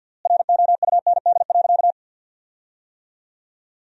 Morse code, heard as a keyed tone, about 50 wpm, 700 Hz. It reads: ROUND.
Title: 3x Word-Spacing - Sentences from Top 200 Words - 50wpm